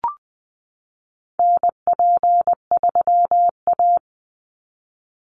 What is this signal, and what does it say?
Morse code, heard as a keyed tone, about 20 wpm, 700 Hz. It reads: NP3A
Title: Call Signs Repeated 3x - 20wpm